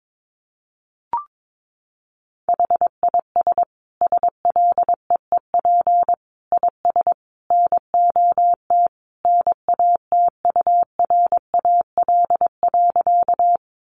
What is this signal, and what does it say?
Morse code, heard as a keyed tone, about 22 wpm, 700 Hz. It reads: HIS SLEEP IS NOT NATURAL.
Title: Sentences from Top 700 Words 22wpm